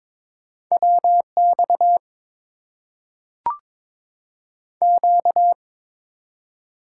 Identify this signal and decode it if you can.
Morse code, heard as a keyed tone, about 22 wpm, 690 Hz. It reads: WX Q
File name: Top 90 QSO Elements - 8x Character Spacing - 22wpm